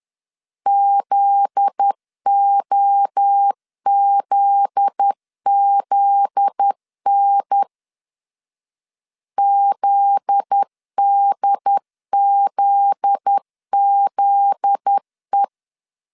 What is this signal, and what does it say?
Morse code, heard as a keyed tone, about 10 wpm, 800 Hz. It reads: ZOZZN ZDZZE
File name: LEZIONI / 7 GF9Z / 7 GF9Z 10wpm 2611